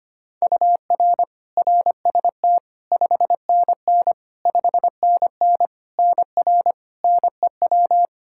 Morse code, heard as a keyed tone, about 25 wpm, 700 Hz. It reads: UR RST 5NN 5NN NR NEW
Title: Practice QSO - Exchange 1 - Beginning - 25wpm